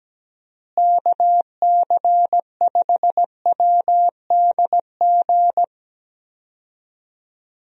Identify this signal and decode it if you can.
Morse code, heard as a keyed tone, about 17 wpm, 700 Hz. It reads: KC5WDG
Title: USA Call Signs - 6 Characters - Repeated 3x - 17wpm